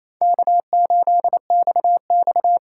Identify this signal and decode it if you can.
Morse code, heard as a keyed tone, about 28 wpm, 700 Hz. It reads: K8XX
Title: USA Call Signs - 4 Characters - 4x Character Spacing - 28wpm